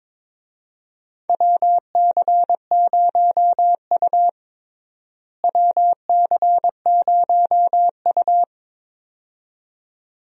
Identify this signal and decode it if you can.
Morse code, heard as a keyed tone, about 22 wpm, 700 Hz. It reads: WC0U WC0U
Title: USA Call Signs - 4 Characters - Repeated 2x - 22wpm